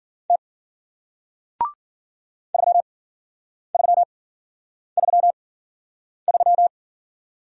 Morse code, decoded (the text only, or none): T 3333